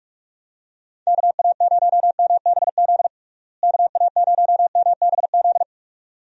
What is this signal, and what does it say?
Morse code, heard as a keyed tone, about 46 wpm, 700 Hz. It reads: KA0MBZ KA0MBZ
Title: USA Call Signs - 6 Characters - Repeated 2x - 45wpm